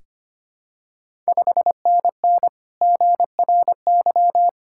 Morse code, heard as a keyed tone, about 25 wpm, 700 Hz. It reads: 5NN GRY
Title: Ontario QSO Party Exchange 25wpm